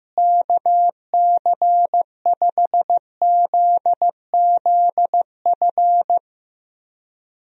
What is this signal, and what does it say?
Morse code, heard as a keyed tone, about 15 wpm, 700 Hz. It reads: KC5ZZF
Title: USA Call Signs - 6 Characters - Repeated 3x - 15wpm